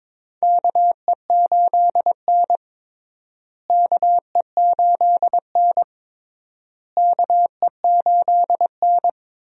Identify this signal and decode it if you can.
Morse code, heard as a keyed tone, about 22 wpm, 700 Hz. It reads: KE8N KE8N KE8N
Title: USA Call Signs - 4 Characters - Repeated 3x - 22wpm